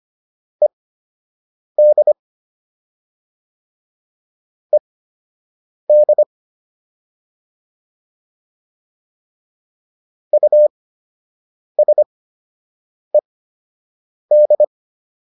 Morse code, heard as a keyed tone, about 25 wpm, spaced as for 7 wpm, 600 Hz. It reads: ED ED USED